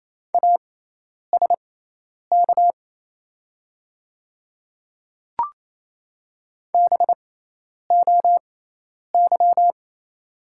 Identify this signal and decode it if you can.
Morse code, heard as a keyed tone, about 28 wpm, spaced as for 10 wpm, 700 Hz. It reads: ASK BOY
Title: 6x Character Spacing - Top 200 Words - 28wpm